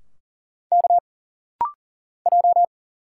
Morse code, heard as a keyed tone, about 39 wpm, 700 Hz. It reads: K J
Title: Single Letters - Rapid-Fire 40wpm